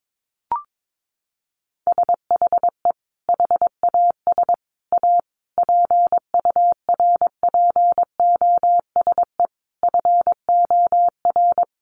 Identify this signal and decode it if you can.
Morse code, heard as a keyed tone, about 22 wpm, 700 Hz. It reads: SHE HAS A PURPOSE FOR
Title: Sentences from Top 800 Words 22wpm